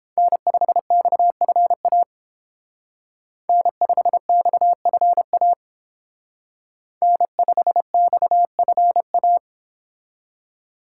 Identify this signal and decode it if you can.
Morse code, heard as a keyed tone, about 29 wpm, 700 Hz. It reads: N5XFA N5XFA N5XFA